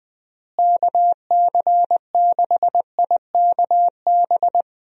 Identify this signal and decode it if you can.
Morse code, heard as a keyed tone, about 20 wpm, 700 Hz. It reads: KC6IKB
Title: USA Call Signs - 6 Characters - Repeated 3x - 20wpm